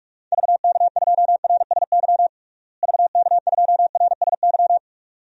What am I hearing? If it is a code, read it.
Morse code, heard as a keyed tone, about 46 wpm, 700 Hz. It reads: VK2RSY VK2RSY
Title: DXCC-100 Call Signs - 6 Characters - Repeated 2x - 45wpm